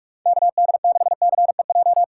Morse code, heard as a keyed tone, about 60 wpm, spaced as for 52 wpm, 700 Hz. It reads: KD6CEJ
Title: Speed Racing - USA 2x3 Call Signs 45wpm